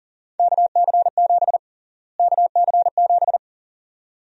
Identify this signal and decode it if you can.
Morse code, heard as a keyed tone, about 38 wpm, 700 Hz. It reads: KC7 KC7
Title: USA Call Sign Prefixes - Repeated 2x - 40wpm